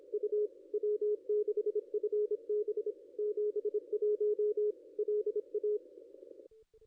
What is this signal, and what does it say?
Morse code, heard as a keyed tone, about 29 wpm, 440 Hz, with radio noise. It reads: U W6FB 71 LA
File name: contests / ss-cw-k5tr-op-n5rz -0400